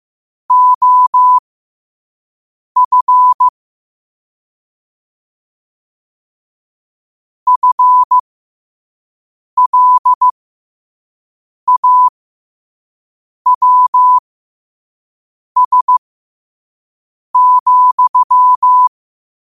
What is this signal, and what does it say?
Morse code, heard as a keyed tone, about 15 wpm, spaced as for 5 wpm, 1000 Hz. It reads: OF FLAWS,
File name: qotdfiles / 5wpm